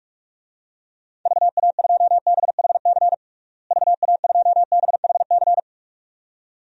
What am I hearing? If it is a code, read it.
Morse code, heard as a keyed tone, about 46 wpm, 700 Hz. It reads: VA2BHC VA2BHC
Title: DXCC-100 Call Signs - 6 Characters - Repeated 2x - 45wpm